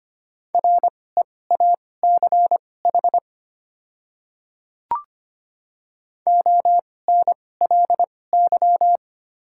Morse code, heard as a keyed tone, about 25 wpm, spaced as for 18 wpm, 690 Hz. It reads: REACH ONLY
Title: 2x Character Spacing - Top 500 Words - 25wpm